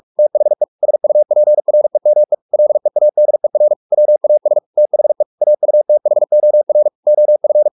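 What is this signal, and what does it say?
Morse code, heard as a keyed tone, about 44 wpm, 600 Hz. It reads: THE SUPREME LEADER WAS THE AUTHOR OF
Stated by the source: Morse practice at 45 wpm, the novel Never by Ken Follett in English, from CW med Gnister.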